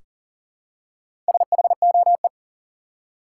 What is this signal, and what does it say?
Morse code, heard as a keyed tone, about 40 wpm, 700 Hz. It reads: SHOE
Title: Four Letter Words - 40wpm